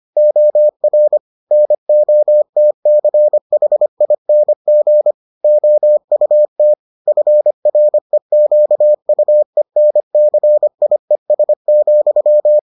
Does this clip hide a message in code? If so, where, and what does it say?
Yes: Morse code from 0.2 to 12.6 s, heard as a keyed tone, about 25 wpm, 600 Hz. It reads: OR NOTCHING OUT FREQUENCIES,